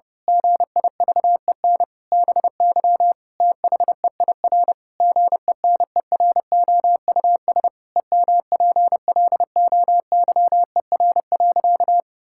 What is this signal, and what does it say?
Morse code, heard as a keyed tone, about 30 wpm, 700 Hz. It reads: GIVEN BY THEIR GENEROUS EMPLOYER.
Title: Pangram Challenge v2 - 30wpm